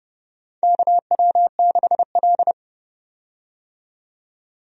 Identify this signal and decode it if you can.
Morse code, heard as a keyed tone, about 30 wpm, 700 Hz. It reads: KW6L
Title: USA Call Signs - 4 Characters - 30wpm